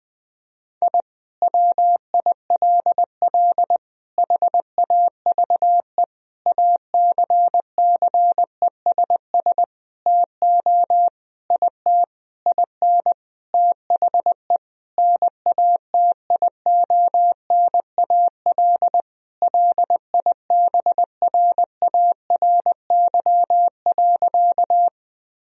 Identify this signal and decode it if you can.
Morse code, heard as a keyed tone, about 20 wpm, 700 Hz. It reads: I WILL HAVE ACCESS TO IT IN THE NATIONAL LIBRARY.
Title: Sentences from Top 1000 Words - Review Part 4 - 20wpm